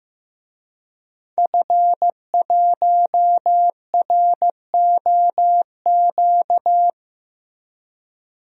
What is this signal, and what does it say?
Morse code, heard as a keyed tone, about 15 wpm, 700 Hz. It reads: F1ROQ
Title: DXCC-100 Call Signs - 5 Characters - Repeated 3x - 15wpm